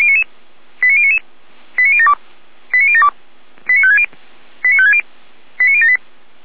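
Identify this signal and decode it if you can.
Five-tone selective call (ZVEI), heard as quick runs of five tones, tones of about 70 ms. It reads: 89099 89072 89072 89560 89560 89078
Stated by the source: ZVEI five-tone call 89150 received by SDR, audio.f85360.feuerwehr-orlamuende.org fire brigade server.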